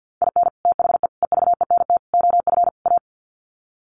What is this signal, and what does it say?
Morse code, heard as a keyed tone, about 50 wpm, 700 Hz. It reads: IN THE EVENT OF A
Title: Sets of 5 Words Encore 50wpm